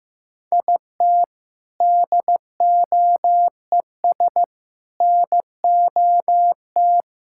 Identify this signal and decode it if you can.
Morse code, heard as a keyed tone, about 15 wpm, 700 Hz. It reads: IT DOES NOT
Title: Sentences from Top 1000 Words - Review Part 6 - 15wpm